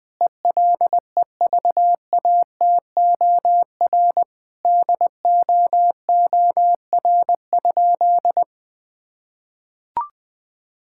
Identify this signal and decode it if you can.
Morse code, heard as a keyed tone, about 21 wpm, 700 Hz. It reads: ELEVATOR DOOR?
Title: Sentences from Top 1600 Words 20wpm